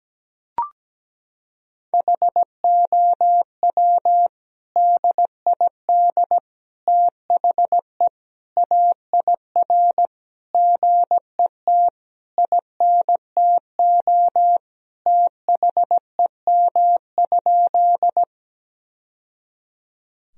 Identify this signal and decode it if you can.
Morse code, heard as a keyed tone, about 17 wpm, 700 Hz. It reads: HOW DID THE AIR GET INTO THEM?
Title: Sentences from Top 1000 Words - Review Part 1 - 17wpm